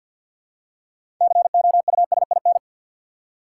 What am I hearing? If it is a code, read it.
Morse code, heard as a keyed tone, about 52 wpm, 700 Hz. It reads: COUSIN
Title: Six Letter Words - 50wpm